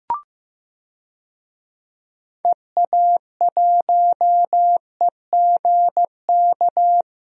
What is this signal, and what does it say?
Morse code, heard as a keyed tone, about 15 wpm, 700 Hz. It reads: EA1EGK
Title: DXCC-100 Call Signs - 6 Characters - Repeated 2x - 15wpm